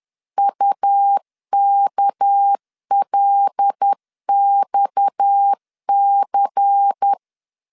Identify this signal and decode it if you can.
Morse code, heard as a keyed tone, about 10 wpm, 800 Hz. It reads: UKLXC